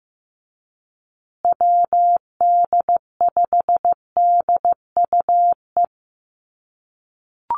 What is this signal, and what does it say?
Morse code, heard as a keyed tone, about 15 wpm, 700 Hz. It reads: WD5DUE
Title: USA Call Signs Repeated 3x 15wpm